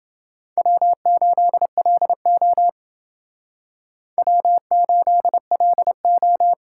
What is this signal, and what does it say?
Morse code, heard as a keyed tone, about 29 wpm, 700 Hz. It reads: W8LO W8LO